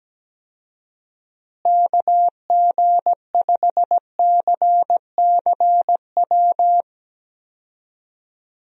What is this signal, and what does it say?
Morse code, heard as a keyed tone, about 17 wpm, 700 Hz. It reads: KG5CCW
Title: USA Call Signs - 6 Characters - Repeated 3x - 17wpm